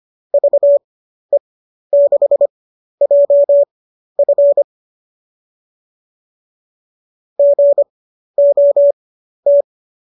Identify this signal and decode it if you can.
Morse code, heard as a keyed tone, about 25 wpm, spaced as for 12 wpm, 590 Hz. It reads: VE6JF GOT